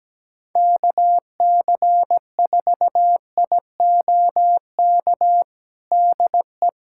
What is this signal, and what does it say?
Morse code, heard as a keyed tone, about 17 wpm, 700 Hz. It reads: KC4IOK DE
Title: USA Callsign DE Callsign - 6 Characters - 17wpm